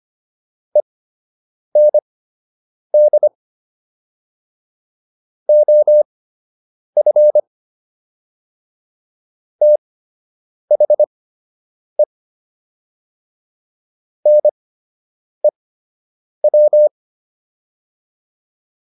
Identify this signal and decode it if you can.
Morse code, heard as a keyed tone, about 25 wpm, spaced as for 8 wpm, 600 Hz. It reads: END OF THE NEW